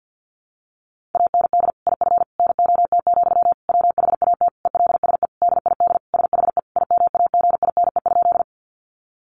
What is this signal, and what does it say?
Morse code, heard as a keyed tone, about 50 wpm, 700 Hz. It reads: AND IF NOT, WHAT ELSE DID SHE IMAGINE?
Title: Sentences from Top 1400 Words 50wpm